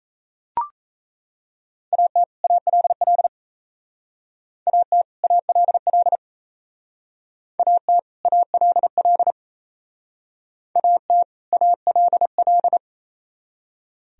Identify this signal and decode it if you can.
Morse code, heard as a keyed tone, about 37 wpm, 700 Hz. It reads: AT ALL AT ALL AT ALL AT ALL